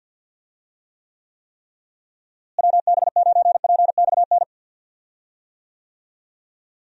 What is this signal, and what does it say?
Morse code, heard as a keyed tone, about 50 wpm, 700 Hz. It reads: WB9PXN
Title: USA Call Signs - 6 Characters - Repeated 3x - 50wpm